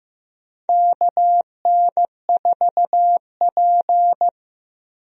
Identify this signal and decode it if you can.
Morse code, heard as a keyed tone, about 15 wpm, 700 Hz. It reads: KN4P
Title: USA Call Signs - 4 Characters - Repeated 3x - 15wpm